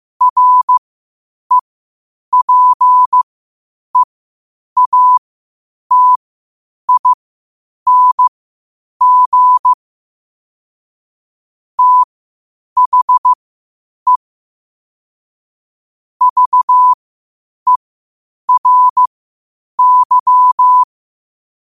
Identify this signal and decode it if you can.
Morse code, heard as a keyed tone, about 15 wpm, spaced as for 8 wpm, 1000 Hz. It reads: REPEATING THE VERY
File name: qotdfiles / 8wpm